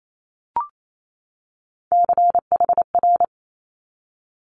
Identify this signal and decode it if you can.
Morse code, heard as a keyed tone, about 28 wpm, 700 Hz. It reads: CHR
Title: Multi-letter Phonemes 28wpm